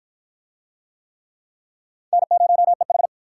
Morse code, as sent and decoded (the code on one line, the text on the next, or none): -. ----- . ....
N0EH